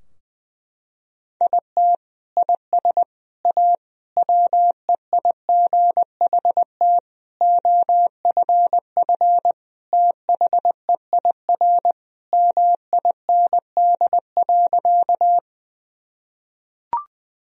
Morse code, heard as a keyed tone, about 20 wpm, 700 Hz. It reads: IT IS A WEIGHT OFF THEIR MIND.
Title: Sentences from Top 1000 Words - Review Part 3 - 20wpm